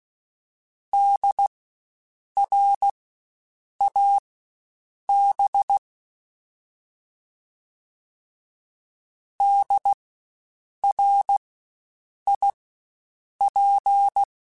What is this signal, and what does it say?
Morse code, heard as a keyed tone, about 16 wpm, spaced as for 7 wpm, 800 Hz. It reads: DRAB DRIP
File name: Spaced Alphas / Spaced Alphas mp3s / Ds sp @15